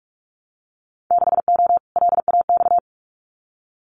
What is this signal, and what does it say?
Morse code, heard as a keyed tone, about 46 wpm, 700 Hz. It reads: 6O LAX
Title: Winter Field Day Exchanges Encore 45wpm